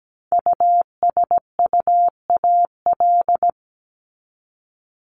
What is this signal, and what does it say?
Morse code, heard as a keyed tone, about 17 wpm, 700 Hz. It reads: USUAL
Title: Top 500 Words Repeated 17wpm